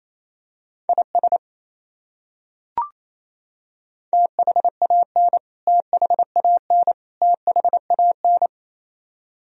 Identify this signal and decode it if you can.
Morse code, heard as a keyed tone, about 28 wpm, 700 Hz. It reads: IS THAN THAN THAN
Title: Top 100 Words Repeated 28wpm